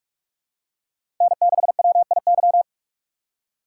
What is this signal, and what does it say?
Morse code, heard as a keyed tone, about 44 wpm, 700 Hz. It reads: N6WIY
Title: USA Call Signs - 4x Character Spacing - 45wpm